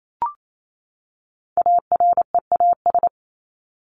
Morse code, heard as a keyed tone, about 28 wpm, 700 Hz. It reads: AREAS
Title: ES:S Suffix - Top 100 Words with Base Words 28wpm